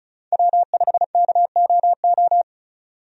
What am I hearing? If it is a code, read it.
Morse code, heard as a keyed tone, about 36 wpm, 700 Hz. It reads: W5KOO
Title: USA Call Signs - 5 Characters - 35wpm